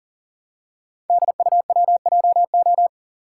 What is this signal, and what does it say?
Morse code, heard as a keyed tone, about 42 wpm, 690 Hz. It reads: DUWJO
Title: Code Groups - 5 Letter:Numbers - 40wpm